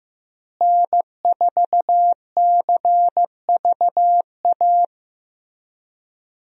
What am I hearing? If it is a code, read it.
Morse code, heard as a keyed tone, about 16 wpm, 700 Hz. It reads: N4CVA
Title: USA Call Signs - 5 Characters - Repeated 3x - 15wpm